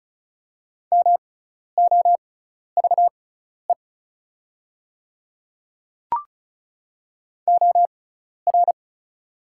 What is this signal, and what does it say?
Morse code, heard as a keyed tone, about 36 wpm, spaced as for 12 wpm, 700 Hz. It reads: MOVE OR
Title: 6x Character Spacing - Top 200 Words - 35wpm